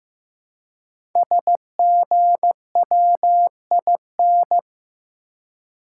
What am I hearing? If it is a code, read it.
Morse code, heard as a keyed tone, about 15 wpm, 700 Hz. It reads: SGWIN
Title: Code Groups - 5 Letters - 15wpm